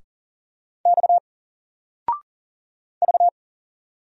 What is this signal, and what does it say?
Morse code, heard as a keyed tone, about 40 wpm, 700 Hz. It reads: X V
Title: Single Letter-Number 40wpm